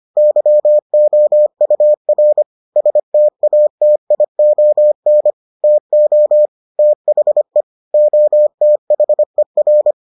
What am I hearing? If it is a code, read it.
Morse code, heard as a keyed tone, about 25 wpm, 600 Hz. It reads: YOUR STATION TO THE OTHER